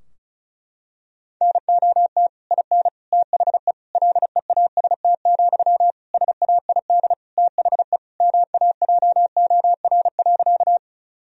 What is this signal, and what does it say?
Morse code, heard as a keyed tone, about 34 wpm, 700 Hz. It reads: NOT IN THE LEAST, SAID THE MAJOR.
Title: Sentences from Top 1000 Words - Review Part 4 - 35wpm